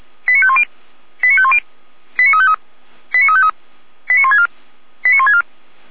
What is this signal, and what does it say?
Five-tone selective call (ZVEI), heard as quick runs of five tones, tones of about 70 ms. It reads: 89510 89510 89353 89353 89164 89164